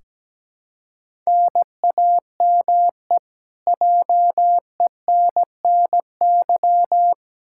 Morse code, heard as a keyed tone, about 17 wpm, 700 Hz. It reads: NAME JENNY
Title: K1USN SST QSO Practice 17wpm